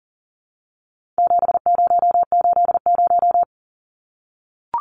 Morse code, heard as a keyed tone, about 38 wpm, 700 Hz. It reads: 7080
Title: Numbers - 4-Digit 40wpm